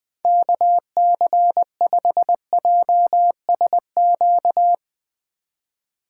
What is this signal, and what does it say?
Morse code, heard as a keyed tone, about 21 wpm, 700 Hz. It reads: KC5JSQ
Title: USA Call Signs - 6 Characters - 2x Character Spacing - 20wpm